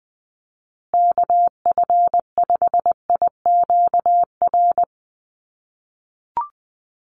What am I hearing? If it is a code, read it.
Morse code, heard as a keyed tone, about 20 wpm, 700 Hz. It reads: KF5IQR
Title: USA Call Signs Repeated 3x 20wpm